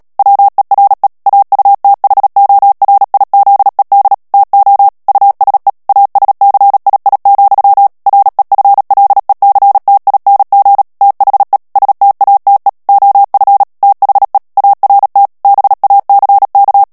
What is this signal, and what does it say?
Morse code, heard as a keyed tone, about 37 wpm, 800 Hz. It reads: WERE AUTHORIZED TO USE ASCII, REFLECTING THE STATE OF THE ART BACK
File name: morse mp3 / ANGLAIS texte clair 3 vitesse 35 mots minute